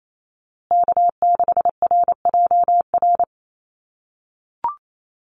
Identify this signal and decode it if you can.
Morse code, heard as a keyed tone, about 28 wpm, 700 Hz. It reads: K6RJR
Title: USA Call Signs Repeated 2x 28wpm